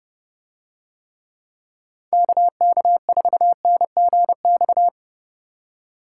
Morse code, heard as a keyed tone, about 29 wpm, 700 Hz. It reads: KK4NGX